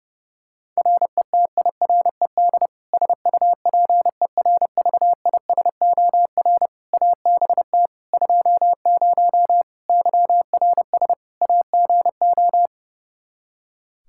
Morse code, heard as a keyed tone, about 30 wpm, 700 Hz. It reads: RETIRED SUPERVISOR ABT 20 YRS AGO
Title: Practice QSO - Exchange 3 - More Info - 30wpm